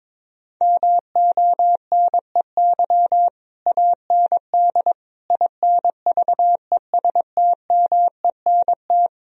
Morse code, heard as a keyed tone, about 22 wpm, 700 Hz. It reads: MONEY AND INVESTMENT
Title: Sentences from Top 1000 Words - Review Part 4 - 22wpm